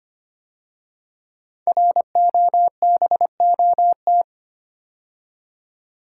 Morse code, heard as a keyed tone, about 25 wpm, 700 Hz. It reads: ROBOT
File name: 5 Letter Words - First 3 Letters Given - 25wpm